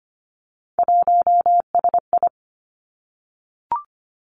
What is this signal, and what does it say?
Morse code, heard as a keyed tone, about 25 wpm, 700 Hz. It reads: 1SI
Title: Three Challenging Letter-Number Combinations 25wpm